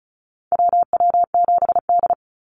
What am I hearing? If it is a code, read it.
Morse code, heard as a keyed tone, about 34 wpm, 700 Hz. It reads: WW7D
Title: NA SOTA Activator Callsigns 35wpm